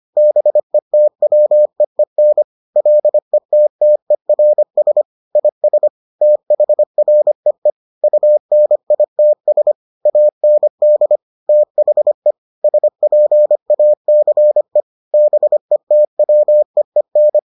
Morse code, heard as a keyed tone, about 25 wpm, 600 Hz. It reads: BETWEEN LETTERS IS THREE UNITS AND THE SPACE BETWEEN